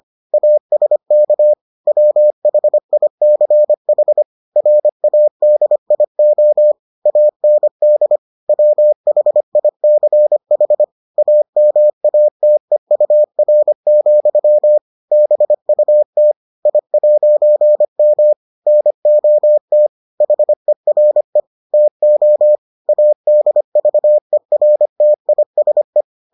Morse code, heard as a keyed tone, about 25 wpm, 600 Hz. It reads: ASK WHICH RADIO AND WHICH AMATEUR, BUT I'M NOT HERE TO ADVERTISE